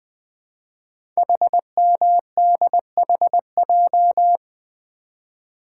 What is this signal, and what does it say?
Morse code, heard as a keyed tone, about 20 wpm, 700 Hz. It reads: HMDHJ
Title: Code Groups - 5 Letters - 20wpm